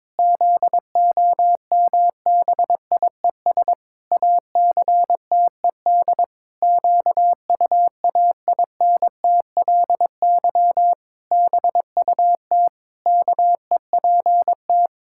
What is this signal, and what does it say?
Morse code, heard as a keyed tone, about 22 wpm, 700 Hz. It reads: ZOMBIES ACTED QUAINTLY BUT KEPT
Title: Pangram Challenge v2 - 22wpm